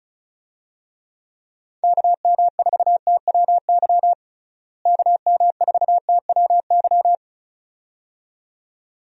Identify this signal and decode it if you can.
Morse code, heard as a keyed tone, about 36 wpm, 700 Hz. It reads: KM4TWY KM4TWY
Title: USA Call Signs - 6 Characters - Repeated 2x - 35wpm